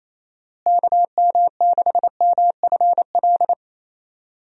Morse code, heard as a keyed tone, about 28 wpm, 700 Hz. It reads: KM6MFL
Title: USA Call Signs - 6 Characters - Repeated 2x - 28wpm